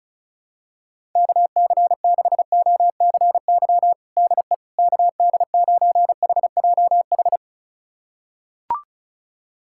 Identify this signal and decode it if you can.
Morse code, heard as a keyed tone, about 34 wpm, 700 Hz. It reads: KC6OCY DE KD9HJH
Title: USA Callsign DE Callsign - 6 Characters - 35wpm